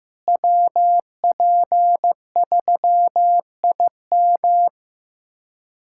Morse code, heard as a keyed tone, about 15 wpm, 700 Hz. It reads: WP3IM
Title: USA Callsign DE Callsign - 5 Characters - 15wpm